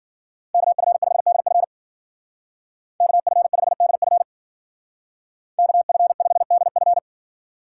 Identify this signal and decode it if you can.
Morse code, heard as a keyed tone, about 52 wpm, 700 Hz. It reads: KF5DF KF5DF KF5DF